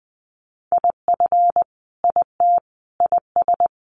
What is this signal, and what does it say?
Morse code, heard as a keyed tone, about 20 wpm, 700 Hz. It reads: IF IT IS
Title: Sentences from Top 1100 Words 20wpm